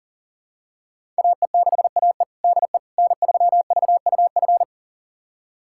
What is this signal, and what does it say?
Morse code, heard as a keyed tone, about 40 wpm, 700 Hz. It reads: AE6AE DE N3VUF
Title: USA Callsign DE Callsign - 5 Characters - 40wpm